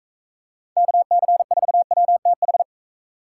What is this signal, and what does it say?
Morse code, heard as a keyed tone, about 33 wpm, 700 Hz. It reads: KC4WTH
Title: Speed Racing - USA 2x3 Call Signs 28wpm